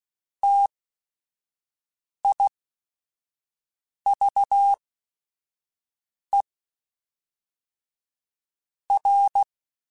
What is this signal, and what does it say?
Morse code, heard as a keyed tone, about 16 wpm, spaced as for 5 wpm, 800 Hz. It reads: TIVE R
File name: Technical Subjects / Technical Subjects-mp3s / XC @5